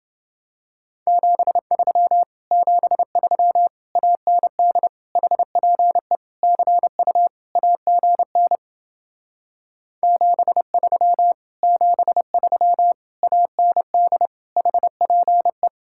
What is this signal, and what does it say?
Morse code, heard as a keyed tone, about 29 wpm, 700 Hz. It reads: 73 73 AND HPE CU AGN 73 73 AND HPE